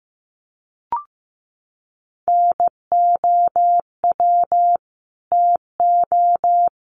Morse code, heard as a keyed tone, about 15 wpm, 700 Hz. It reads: NOW TO
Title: Sentences from Top 1300 Words 15wpm